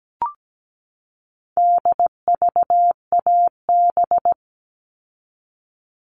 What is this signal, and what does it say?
Morse code, heard as a keyed tone, about 17 wpm, 700 Hz. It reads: DVAB
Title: All Cut Numbers - 4-Digit 17wpm